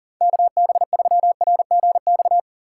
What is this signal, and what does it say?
Morse code, heard as a keyed tone, about 40 wpm, 700 Hz. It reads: KB3RGX
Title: USA Call Signs - 3x Character Spacing - 40wpm